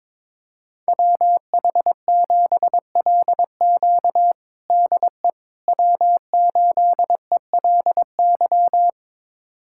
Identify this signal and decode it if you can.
Morse code, heard as a keyed tone, about 22 wpm, 700 Hz. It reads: WH7LQ DE W8ELY
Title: USA Callsign DE Callsign - 5 Characters - 22wpm